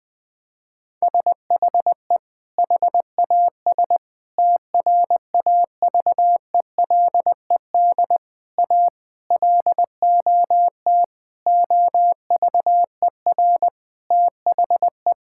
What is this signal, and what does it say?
Morse code, heard as a keyed tone, about 21 wpm, 700 Hz. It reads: SHE HAS TRAVELED A LOT OVER THE
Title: Sentences with 6-Word Phrases - 20wpm